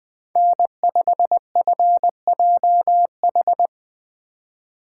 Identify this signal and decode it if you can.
Morse code, heard as a keyed tone, about 22 wpm, 700 Hz. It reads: N5FJH
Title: Speed Racing - USA 1x3 Call Signs 20wpm